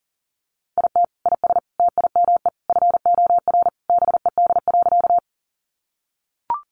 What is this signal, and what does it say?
Morse code, heard as a keyed tone, about 40 wpm, 700 Hz. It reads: IT IS TIME FOR BED.
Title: Sentences from Top 500 Words 40wpm